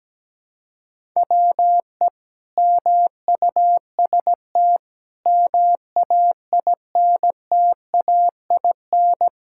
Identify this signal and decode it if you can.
Morse code, heard as a keyed tone, about 17 wpm, 700 Hz. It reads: WE MUST MAINTAIN
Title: Sentences from Top 1000 Words - Review Part 5 - 17wpm